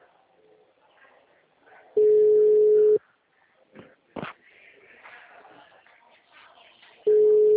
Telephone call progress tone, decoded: ringback tone

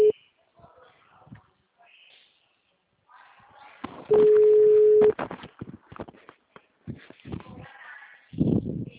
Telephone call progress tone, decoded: ringback tone

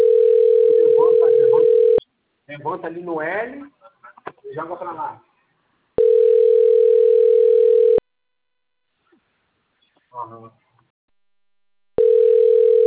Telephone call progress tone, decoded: ringback tone

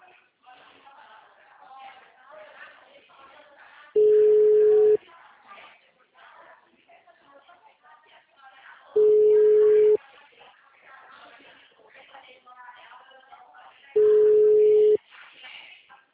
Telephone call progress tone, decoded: ringback tone

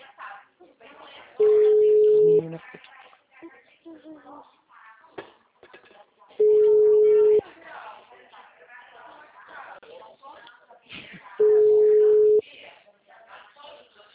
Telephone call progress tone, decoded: ringback tone

